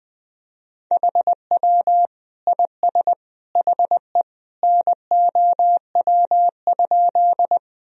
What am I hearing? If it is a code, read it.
Morse code, heard as a keyed tone, about 20 wpm, 700 Hz. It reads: HW IS HE NOW?
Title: Sentences from Top 100 Words and Abbreviations 20wpm